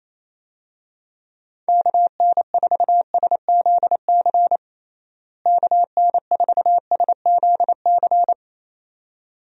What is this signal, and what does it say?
Morse code, heard as a keyed tone, about 28 wpm, 700 Hz. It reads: KN4SZC KN4SZC
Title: USA Call Signs - 6 Characters - Repeated 2x - 28wpm